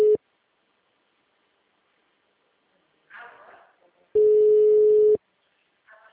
Ringback tone (Europe, 425 Hz).